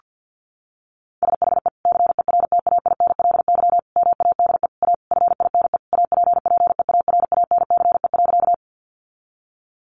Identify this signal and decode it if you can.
Morse code, heard as a keyed tone, about 50 wpm, 700 Hz. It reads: SHE CERTAINLY MADE A FINE APPEARANCE.
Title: Sentences from Top 1100 Words 50wpm